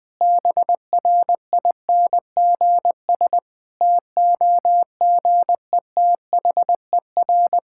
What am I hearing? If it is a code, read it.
Morse code, heard as a keyed tone, about 20 wpm, 700 Hz. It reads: BRINGS TOGETHER
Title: Difficult Sentences with Hints - 20wpm